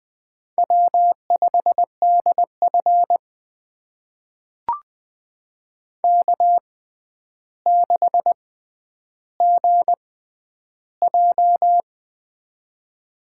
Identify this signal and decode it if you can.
Morse code, heard as a keyed tone, about 20 wpm, spaced as for 10 wpm, 700 Hz. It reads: W5DF K6GJ
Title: USA Call Signs - 4 Characters - 6x Character Spacing - 20wpm